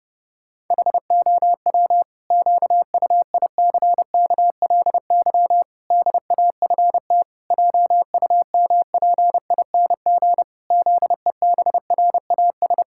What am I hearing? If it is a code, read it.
Morse code, heard as a keyed tone, about 30 wpm, 700 Hz. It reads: HOW QUICKLY DAFT JUMPING ZEBRAS
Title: Pangram Challenge v2 - 30wpm